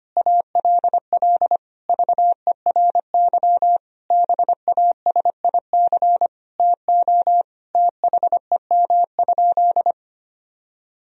Morse code, heard as a keyed tone, about 25 wpm, 700 Hz. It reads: ALL VERY BASIC TO THEM?
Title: Sentences from Top 1000 Words - Review Part 3 - 25wpm